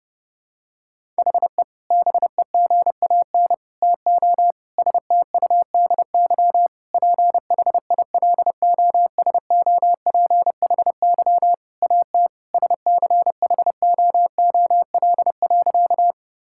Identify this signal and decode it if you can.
Morse code, heard as a keyed tone, about 30 wpm, 700 Hz. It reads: HE BEGAN TO STUDY PHILOSOPHY AT SCHOOL.
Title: Sentences from Top 1000 Words - Review Part 6 - 30wpm